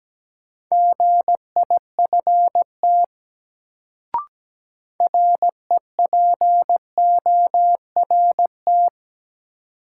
Morse code, heard as a keyed tone, about 17 wpm, 700 Hz. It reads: GIFT REPORT
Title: Sentences from Top 900 Words - New Word Drill - 17wpm